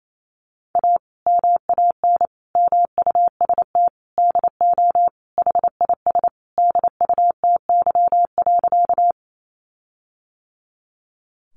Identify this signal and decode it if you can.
Morse code, heard as a keyed tone, about 28 wpm, 700 Hz. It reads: A MAN MUST DO HIS DUTY.
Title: Sentences from Top 1500 Words 28wpm